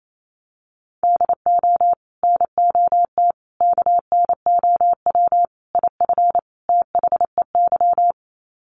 Morse code, heard as a keyed tone, about 28 wpm, 700 Hz. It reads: DO NOT KNOW IF THEY
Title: Sets of 5 Words Encore 28wpm